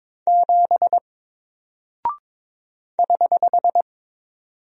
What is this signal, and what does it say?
Morse code, heard as a keyed tone, about 22 wpm, 700 Hz. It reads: 7 <HH>
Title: Single Character (Letter:Number:Punctuation:Prosign) Alt 22wpm